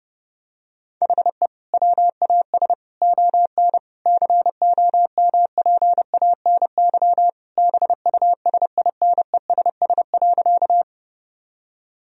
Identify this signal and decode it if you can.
Morse code, heard as a keyed tone, about 30 wpm, 700 Hz. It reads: HE WAS ON COMPANY BUSINESS.
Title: Sentences from Top 1000 Words - Review Part 3 - 30wpm